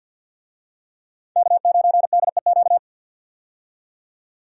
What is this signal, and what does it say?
Morse code, heard as a keyed tone, about 50 wpm, 700 Hz. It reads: K9DEQ
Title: USA Call Signs - 5 Characters - 50wpm